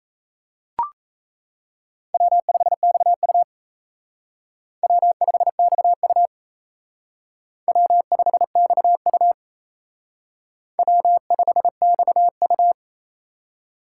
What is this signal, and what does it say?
Morse code, heard as a keyed tone, about 34 wpm, 700 Hz. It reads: W5XU W5XU W5XU W5XU
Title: Speed Racing - USA 1x2 Call Signs 28wpm